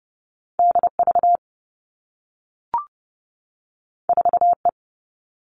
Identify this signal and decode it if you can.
Morse code, heard as a keyed tone, about 30 wpm, 700 Hz. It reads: D V 4 E
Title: Two Challenging Letter-Number Combinations 30wpm